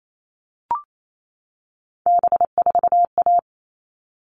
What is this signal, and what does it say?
Morse code, heard as a keyed tone, about 28 wpm, 700 Hz. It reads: B4A
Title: All Cut Numbers - 3-Digit 28wpm